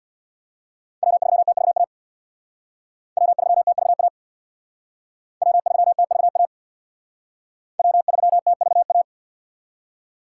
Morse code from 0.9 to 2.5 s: W3TVA

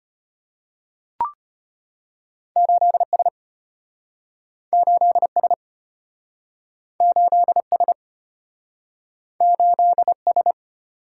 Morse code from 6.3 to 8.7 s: 8S

